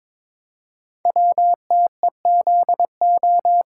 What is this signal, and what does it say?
Morse code, heard as a keyed tone, about 22 wpm, 700 Hz. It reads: WTEZO
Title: Code Groups - 5 Letters - 22wpm